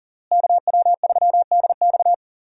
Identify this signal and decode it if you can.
Morse code, heard as a keyed tone, about 40 wpm, 700 Hz. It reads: KW3DX
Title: USA Call Signs - 5 Characters - Repeated 3x - 40wpm